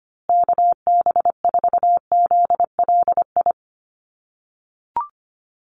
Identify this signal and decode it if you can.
Morse code, heard as a keyed tone, about 25 wpm, 700 Hz. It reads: KB4ZLI
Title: USA Call Signs Repeated 3x 25wpm